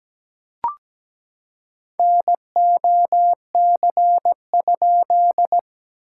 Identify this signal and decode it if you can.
Morse code, heard as a keyed tone, about 17 wpm, 700 Hz. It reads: NOC?
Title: SOTA QSO Practice 17wpm